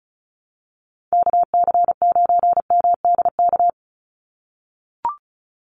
Morse code, heard as a keyed tone, about 34 wpm, 700 Hz. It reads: KC9MDK
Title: USA Call Signs 35wpm